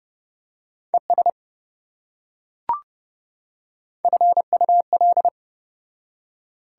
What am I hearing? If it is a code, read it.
Morse code, heard as a keyed tone, about 28 wpm, 700 Hz. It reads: ES FUL